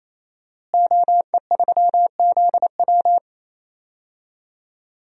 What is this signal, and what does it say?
Morse code, heard as a keyed tone, about 28 wpm, 700 Hz. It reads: OE3ZW
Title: DXCC-100 Call Signs - 5 Characters - Repeated 3x - 28wpm